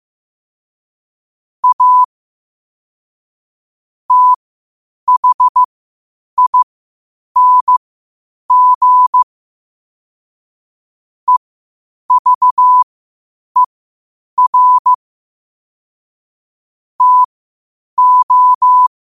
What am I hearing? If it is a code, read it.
Morse code, heard as a keyed tone, about 15 wpm, spaced as for 8 wpm, 1000 Hz. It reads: A THING EVER TO
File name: qotdfiles / 8wpm